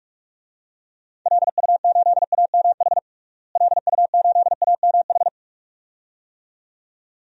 Morse code, heard as a keyed tone, about 46 wpm, 700 Hz. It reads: LU8AMH LU8AMH
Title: DXCC-100 Call Signs - 6 Characters - Repeated 2x - 45wpm